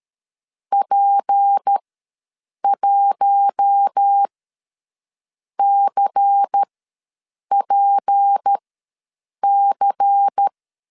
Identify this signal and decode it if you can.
Morse code, heard as a keyed tone, about 13 wpm, spaced as for 7 wpm, 800 Hz. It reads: P1CPC
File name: LEZIONI / 6 PKX0 / 6 P ws 12wpm 3117